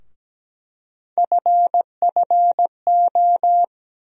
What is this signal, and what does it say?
Morse code, heard as a keyed tone, about 17 wpm, 700 Hz. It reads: FFO
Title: USA Call Sign Postfixes - 3 Letters 17wpm